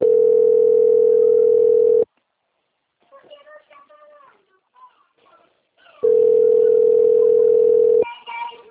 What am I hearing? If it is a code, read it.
Ringback tone (North America), 440 + 480 Hz.
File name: out- - - - - .32628